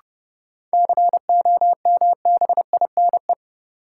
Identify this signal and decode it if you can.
Morse code, heard as a keyed tone, about 30 wpm, 700 Hz. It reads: COMBINE
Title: Sentences from Top 1600 Words - New Word Drill - 30wpm